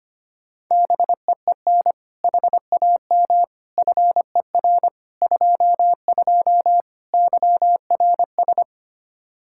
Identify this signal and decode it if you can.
Morse code, heard as a keyed tone, about 25 wpm, 700 Hz. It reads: BEEN HAM FER 22 YRS
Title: Practice QSO - Exchange 3 - More Info - 25wpm